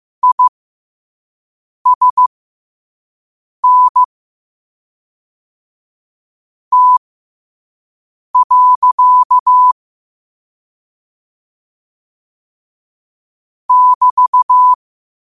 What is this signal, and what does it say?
Morse code, heard as a keyed tone, about 15 wpm, spaced as for 5 wpm, 1000 Hz. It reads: ISN T. <BT>
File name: qotdfiles / 5wpm